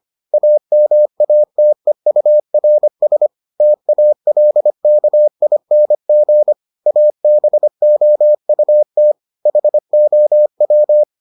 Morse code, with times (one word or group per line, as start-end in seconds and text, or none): 0.3-3.3 AMATEURS
3.6-6.5 TALKING
6.9-9.1 ABOUT
9.5-11.0 HOW